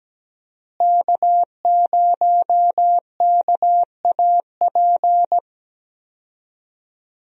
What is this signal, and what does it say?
Morse code, heard as a keyed tone, about 17 wpm, 700 Hz. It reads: K0KAP
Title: USA Call Signs - 5 Characters - Repeated 3x - 17wpm